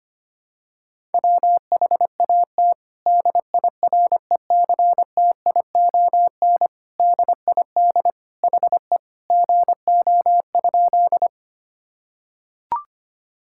Morse code, heard as a keyed tone, about 25 wpm, 700 Hz. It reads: WHAT DIRECTION DID HE GO?
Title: Sentences from Top 1000 Words - Review Part 5 - 25wpm